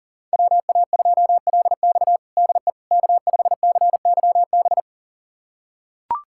Morse code, heard as a keyed tone, about 37 wpm, 700 Hz. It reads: WA2LX DE K5CYB
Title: USA Callsign DE Callsign - 5 Characters - 40wpm